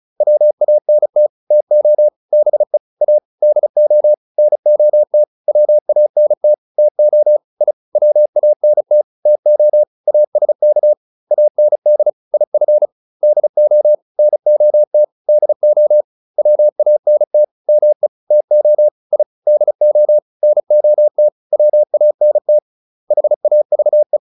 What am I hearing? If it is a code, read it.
Morse code, heard as a keyed tone, about 34 wpm, 600 Hz. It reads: WANT TO BE A DO NOT WANT TO I WANT TO ASK AND IF DO NOT DO WANT ME TO I DO NOT WANT HAVE